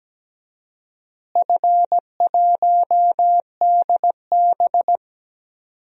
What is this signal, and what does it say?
Morse code, heard as a keyed tone, about 17 wpm, 700 Hz. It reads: F1DB
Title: DXCC-100 Call Signs - 4 Characters - Repeated 2x - 17wpm